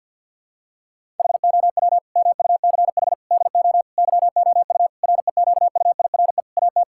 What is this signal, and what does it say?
Morse code, heard as a keyed tone, about 52 wpm, 700 Hz. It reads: HOW MUCH DO YOU REQUIRE AT
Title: Sentences from Top 1000 Words - Review Part 6 - 50wpm